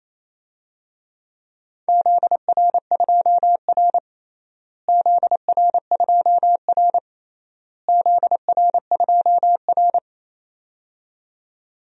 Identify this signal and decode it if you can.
Morse code, heard as a keyed tone, about 27 wpm, 700 Hz. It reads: ZR2R ZR2R ZR2R
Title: DXCC-100 Call Signs - 4 Characters - Repeated 3x - 28wpm